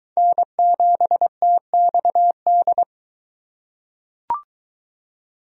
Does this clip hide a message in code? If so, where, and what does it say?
Yes: Morse code from 0.2 to 2.8 s, heard as a keyed tone, about 21 wpm, 700 Hz. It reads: N7TXD